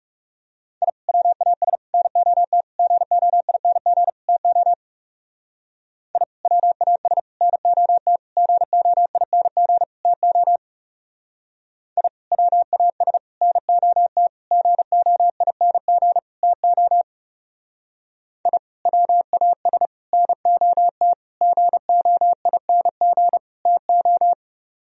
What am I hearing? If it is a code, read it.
Morse code, heard as a keyed tone, about 38 wpm, 700 Hz. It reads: I WAS NOT GOING TO I WAS NOT GOING TO I WAS NOT GOING TO I WAS NOT GOING TO